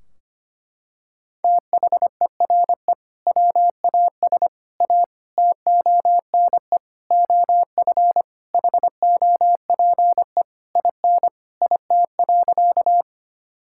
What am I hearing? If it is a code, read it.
Morse code, heard as a keyed tone, about 25 wpm, 700 Hz. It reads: THERE WAS A TONE OF HOPE IN IT.
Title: Sentences from Top 1600 Words 25wpm